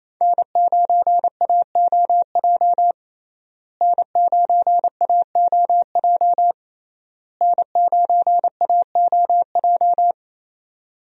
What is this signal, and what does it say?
Morse code, heard as a keyed tone, about 28 wpm, 700 Hz. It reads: N9AOJ N9AOJ N9AOJ
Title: USA Call Signs - 5 Characters - Repeated 3x - 28wpm